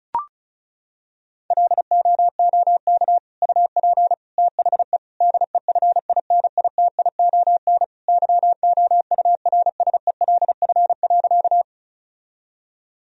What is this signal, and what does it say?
Morse code, heard as a keyed tone, about 36 wpm, 700 Hz. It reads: LOOK UP THE DEFINITION YOURSELF.
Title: Sentences from Top 1000 Words - Review Part 4 - 35wpm